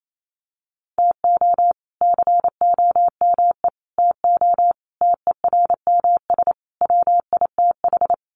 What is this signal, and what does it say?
Morse code, heard as a keyed tone, about 28 wpm, 700 Hz. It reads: TO COME TO TERMS WITH
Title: Sets of 5 Words Encore 28wpm